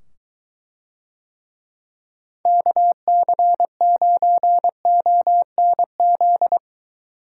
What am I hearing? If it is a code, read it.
Morse code, heard as a keyed tone, about 20 wpm, 700 Hz. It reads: KC9ONZ